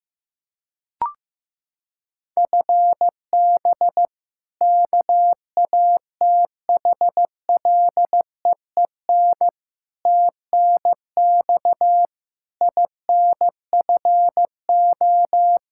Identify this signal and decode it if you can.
Morse code, heard as a keyed tone, about 15 wpm, 700 Hz. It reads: FB KATHLEEN TNX INFO
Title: Practice QSO - Exchange 3 - More Info - 15wpm